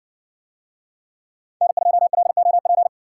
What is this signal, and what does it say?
Morse code, heard as a keyed tone, about 60 wpm, 700 Hz. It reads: N2LOP